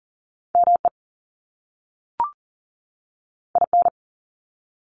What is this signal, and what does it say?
Morse code, heard as a keyed tone, about 42 wpm, 700 Hz. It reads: ME IN
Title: US States 40wpm